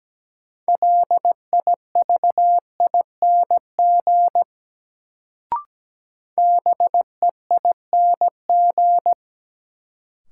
Morse code, heard as a keyed tone, about 17 wpm, 700 Hz. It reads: LIVING BEING
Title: Sets of 4 Words - New Word Drill - 17wpm